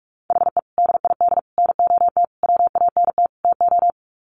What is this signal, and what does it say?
Morse code, heard as a keyed tone, about 44 wpm, 700 Hz. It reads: HE DID NOT WANT TO
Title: Sets of 5 Words 45wpm